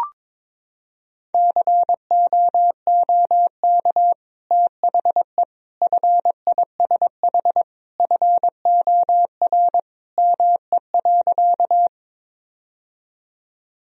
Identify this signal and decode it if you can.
Morse code, heard as a keyed tone, about 22 wpm, 700 Hz. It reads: COOK THE FISH FOR ME.
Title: Sentences from Top 1000 Words - Review Part 4 - 22wpm